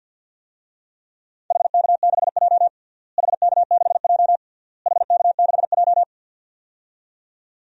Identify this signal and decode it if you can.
Morse code, heard as a keyed tone, about 51 wpm, 700 Hz. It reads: HK6J HK6J HK6J
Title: DXCC-100 Call Signs - 4 Characters - Repeated 3x - 50wpm